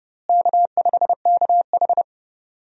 Morse code, heard as a keyed tone, about 30 wpm, 700 Hz. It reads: K5KH
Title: USA Call Signs - 4 Characters - Repeated 2x - 30wpm